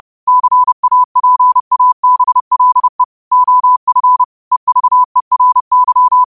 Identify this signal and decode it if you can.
Morse code, heard as a keyed tone, about 30 wpm, 1000 Hz. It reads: CAPABLE OF EVERY